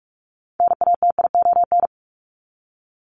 Morse code, heard as a keyed tone, about 44 wpm, 700 Hz. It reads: NATION
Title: ES:S Suffix - Top 100 Words with Base Words 45wpm